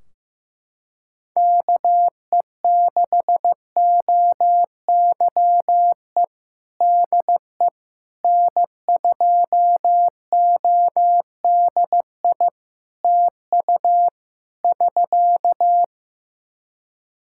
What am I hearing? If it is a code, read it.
Morse code, heard as a keyed tone, about 15 wpm, 700 Hz. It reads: KE6OYE DE N2ODI TU <SK>
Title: K1USN SST QSO Practice 15wpm